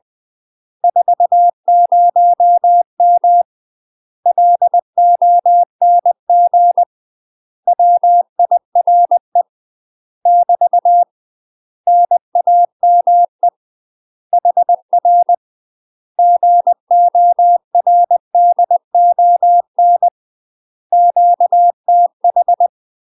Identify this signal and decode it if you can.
Morse code, heard as a keyed tone, about 20 wpm, 700 Hz. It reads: 40M LONG WIRE <BT> NAME HR GORDON QTH
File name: UK MASTER QSOs TRAINING/20WPM